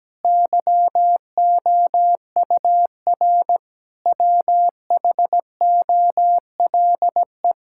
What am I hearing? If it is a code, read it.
Morse code, heard as a keyed tone, about 17 wpm, 700 Hz. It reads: YOUR WHOLE
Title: Sentences from Top 1000 Words - Review Part 5 - 17wpm